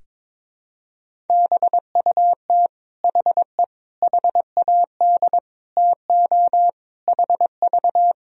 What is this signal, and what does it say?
Morse code, heard as a keyed tone, about 22 wpm, 700 Hz. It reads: BUT HE HAD TO HV
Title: Sentences from Top 100 Words and Abbreviations 22wpm